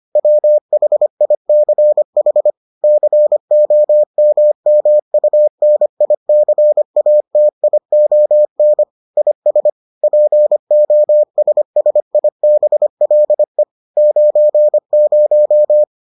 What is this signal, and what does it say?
Morse code, heard as a keyed tone, about 25 wpm, 600 Hz. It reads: WHICH COMMUNICATION IS POSSIBLE 90